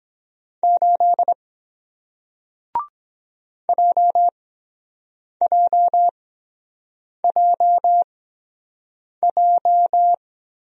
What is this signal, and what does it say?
Morse code, heard as a keyed tone, about 23 wpm, spaced as for 7 wpm, 700 Hz. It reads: 8 JJJJ